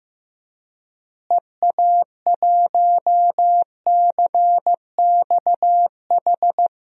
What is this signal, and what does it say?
Morse code, heard as a keyed tone, about 15 wpm, 700 Hz. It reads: EA1CXH
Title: DXCC-100 Call Signs - 6 Characters - Repeated 3x - 15wpm